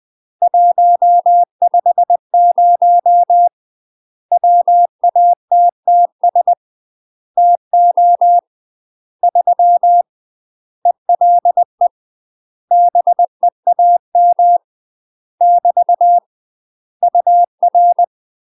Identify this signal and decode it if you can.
Morse code, heard as a keyed tone, about 20 wpm, 700 Hz. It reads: 150 WATTS TO 3 ELE BEAM <BT> UR
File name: UK MASTER QSOs TRAINING / 20WPM / UK MASTER QSOs 0034